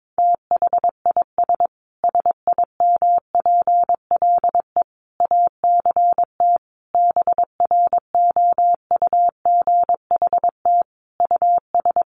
Morse code, heard as a keyed tone, about 22 wpm, 700 Hz. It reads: THIS SIMPLE ACT BROUGHT US
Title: Sentences from Top 1500 Words 22wpm